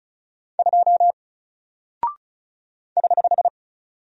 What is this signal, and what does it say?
Morse code, heard as a keyed tone, about 36 wpm, 700 Hz. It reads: 2 <HH>
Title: Single Character (Letter:Number:Punctuation:Prosign) Alt 35wpm